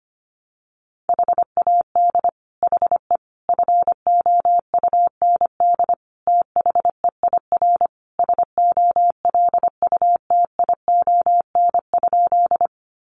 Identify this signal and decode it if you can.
Morse code, heard as a keyed tone, about 25 wpm, 700 Hz. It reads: HAD HE FOUND THEIR SOLUTION?
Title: Sentences from Top 1300 Words 25wpm